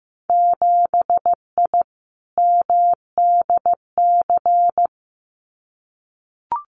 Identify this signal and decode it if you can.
Morse code, heard as a keyed tone, about 15 wpm, 700 Hz. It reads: 7I MDC
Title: Winter Field Day Exchanges Encore 15wpm